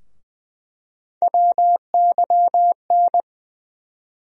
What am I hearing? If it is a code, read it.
Morse code, heard as a keyed tone, about 20 wpm, 700 Hz. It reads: WYN
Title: USA Call Sign Postfixes - 3 Letters 20wpm